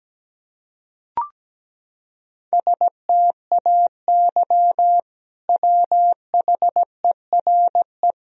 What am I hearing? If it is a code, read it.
Morse code, heard as a keyed tone, about 17 wpm, 700 Hz. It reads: STAY WHERE
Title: Sentences from Top 500 Words and Abbreviations 17wpm